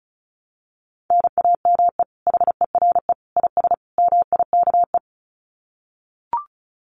Morse code, heard as a keyed tone, about 34 wpm, 700 Hz. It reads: NAME HERE IS MIKE
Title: Common QSO Phrases 35wpm